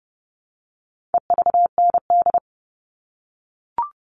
Morse code, heard as a keyed tone, about 30 wpm, 700 Hz. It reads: EVND
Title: All Cut Numbers - 4-Digit 30wpm